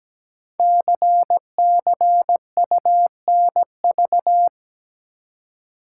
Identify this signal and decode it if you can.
Morse code, heard as a keyed tone, about 17 wpm, 700 Hz. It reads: CCUNV
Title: Code Groups - 5 Letters - 17wpm